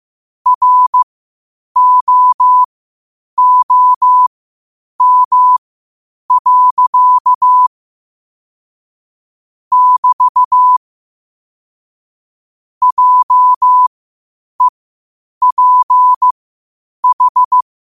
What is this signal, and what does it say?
Morse code, heard as a keyed tone, about 15 wpm, spaced as for 8 wpm, 1000 Hz. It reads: ROOM. <BT> JEPH
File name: qotdfiles / 8wpm